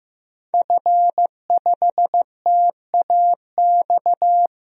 Morse code, heard as a keyed tone, about 15 wpm, 700 Hz. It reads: F5TAX
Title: DXCC-100 Call Signs - 5 Characters - Repeated 2x - 15wpm